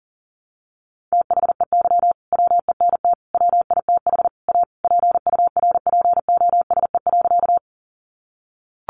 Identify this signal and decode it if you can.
Morse code, heard as a keyed tone, about 40 wpm, 700 Hz. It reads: THEY WENT WITH A PURPOSE.
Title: Sentences from Top 800 Words 40wpm